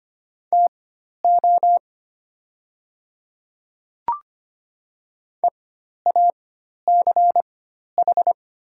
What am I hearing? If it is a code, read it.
Morse code, heard as a keyed tone, about 25 wpm, spaced as for 12 wpm, 700 Hz. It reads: TO EACH
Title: 4x Character Spacing - Top 100 Words - 25wpm